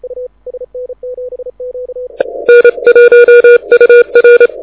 Morse code, heard as a keyed tone, about 27 wpm, 2400 Hz. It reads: N1UR
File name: audio zawody / IARU 2015 CW SN7Q